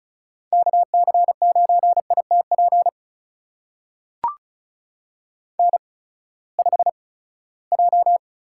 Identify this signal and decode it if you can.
Morse code, heard as a keyed tone, about 34 wpm, spaced as for 20 wpm, 700 Hz. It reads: KC9ITP N5J
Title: USA Call Signs - 8x Character Spacing - 35wpm